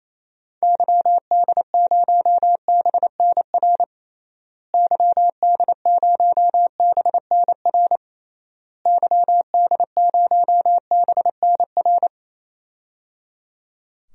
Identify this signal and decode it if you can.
Morse code, heard as a keyed tone, about 28 wpm, 700 Hz. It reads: YD0BNR YD0BNR YD0BNR
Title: DXCC-100 Call Signs - 6 Characters - Repeated 3x - 28wpm